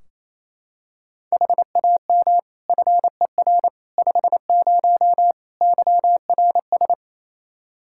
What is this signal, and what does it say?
Morse code, heard as a keyed tone, about 28 wpm, 700 Hz. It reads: HAM FER 50 YRS
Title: Practice QSO - Exchange 3 - More Info - 28wpm